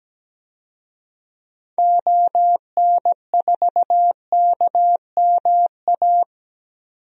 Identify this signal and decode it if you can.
Morse code, heard as a keyed tone, about 17 wpm, 700 Hz. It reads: ON4KMA
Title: DXCC-100 Call Signs - 6 Characters - 17wpm